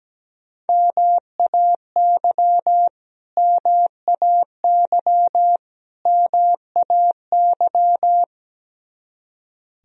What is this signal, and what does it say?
Morse code, heard as a keyed tone, about 17 wpm, 700 Hz. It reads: MAY MAY MAY
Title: Top 100 Words Repeated 17wpm